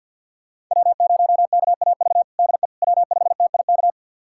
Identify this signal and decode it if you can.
Morse code, heard as a keyed tone, about 52 wpm, 700 Hz. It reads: W0KAV DE W5TIK
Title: USA Callsign DE Callsign - 5 Characters - 50wpm